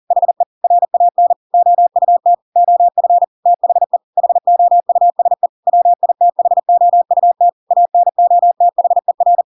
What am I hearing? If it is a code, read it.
Morse code, heard as a keyed tone, about 40 wpm, 700 Hz. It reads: HE RAN OUT OF THE HOUSE WITHOUT ANOTHER
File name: Pinocchio / Pinocchio ch8 40 40 ik1hge